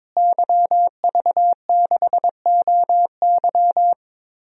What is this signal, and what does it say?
Morse code, heard as a keyed tone, about 22 wpm, 700 Hz. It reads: YV6OY
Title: DXCC-100 Call Signs - 5 Characters - Repeated 2x - 22wpm